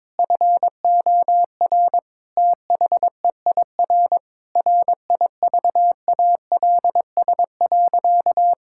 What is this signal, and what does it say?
Morse code, heard as a keyed tone, about 22 wpm, 700 Hz. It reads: FOR THEIR RIVALS.
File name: Sentences with 7-Word Phrases - 22wpm